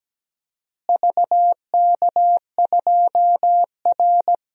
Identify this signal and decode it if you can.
Morse code, heard as a keyed tone, about 17 wpm, 700 Hz. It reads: VK2R
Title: DXCC-100 Call Signs - 4 Characters - Repeated 3x - 17wpm